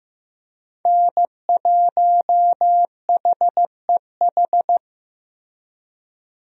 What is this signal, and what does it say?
Morse code, heard as a keyed tone, about 15 wpm, 700 Hz. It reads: N1HEH
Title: USA Call Signs - 5 Characters - Repeated 2x - 15wpm